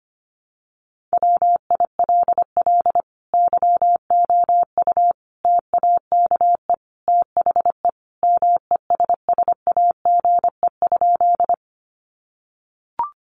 Morse code, heard as a keyed tone, about 25 wpm, 700 Hz. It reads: WILL YOU TAKE THE MESSAGE?
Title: Sentences from Top 1100 Words 25wpm